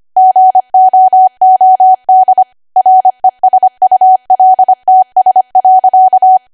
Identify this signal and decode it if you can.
Morse code, heard as a keyed tone, about 25 wpm, 750 Hz. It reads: GOOD RESULTS.
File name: cwa / W1AW-Level3 / 25WPM